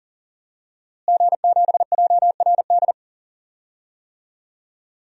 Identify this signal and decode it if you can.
Morse code, heard as a keyed tone, about 40 wpm, 700 Hz. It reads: G7JRD
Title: DXCC-100 Call Signs - 5 Characters - Repeated 3x - 40wpm